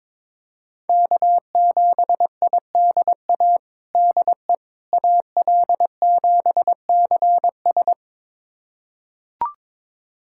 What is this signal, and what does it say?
Morse code, heard as a keyed tone, about 22 wpm, 700 Hz. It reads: K7IDA DE AL7CS
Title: USA Callsign DE Callsign - 5 Characters - 22wpm